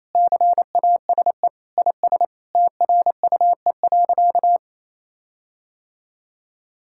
Morse code, heard as a keyed tone, about 28 wpm, 700 Hz. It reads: CASE IS TRUE.
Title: Sentences from Top 1600 Words 28wpm